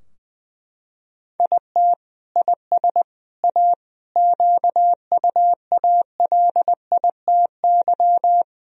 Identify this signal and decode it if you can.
Morse code, heard as a keyed tone, about 19 wpm, 700 Hz. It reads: IT IS A QUALITY
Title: Sentences from Top 1000 Words - Review Part 4 - 20wpm